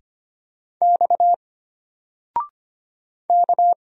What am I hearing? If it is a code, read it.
Morse code, heard as a keyed tone, about 25 wpm, 700 Hz. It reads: X K
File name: Single-Letter-Number-25wpm